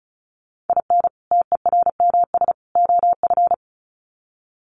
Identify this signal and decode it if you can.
Morse code, heard as a keyed tone, about 34 wpm, 700 Hz. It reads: IN TERMS OF
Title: Sets of 3 Words 35wpm